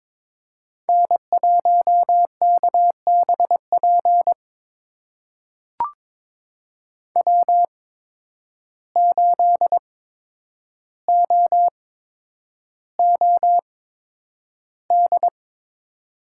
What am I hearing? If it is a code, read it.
Morse code, heard as a keyed tone, about 22 wpm, spaced as for 9 wpm, 700 Hz. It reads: N1KBP W8OOD
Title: USA Call Signs - 5 Characters - 8x Character Spacing - 22wpm